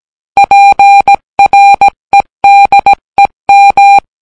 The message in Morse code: PREDEM